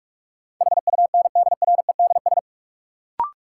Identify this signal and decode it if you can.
Morse code, heard as a keyed tone, about 46 wpm, 700 Hz. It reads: HUNDREDS
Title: Sets of 4 Words - New Word Drill - 45wpm